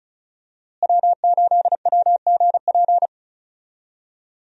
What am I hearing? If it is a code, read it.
Morse code, heard as a keyed tone, about 36 wpm, 700 Hz. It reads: W8WGP
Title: USA Call Signs - 5 Characters - Repeated 3x - 35wpm